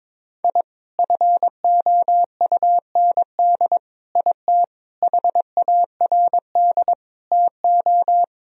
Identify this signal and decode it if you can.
Morse code, heard as a keyed tone, about 22 wpm, 700 Hz. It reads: I FOUND IT HARD TO
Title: Sentences from Top 1000 Words - Review Part 1 - 22wpm